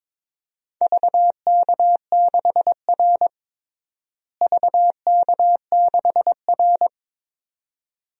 Morse code, heard as a keyed tone, about 22 wpm, 700 Hz. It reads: VK6R VK6R
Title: DXCC-100 Call Signs - 4 Characters - Repeated 2x - 22wpm